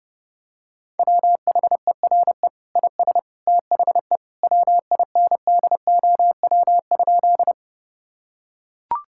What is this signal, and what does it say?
Morse code, heard as a keyed tone, about 30 wpm, 700 Hz. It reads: WHERE IS THE WINDOW?
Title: Sentences from Top 1600 Words 30wpm